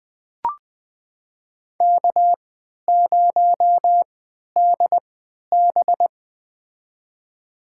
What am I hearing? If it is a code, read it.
Morse code, heard as a keyed tone, about 20 wpm, spaced as for 11 wpm, 700 Hz. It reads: K0DB
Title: USA Call Signs - 4 Characters - 3x Character Spacing - 20wpm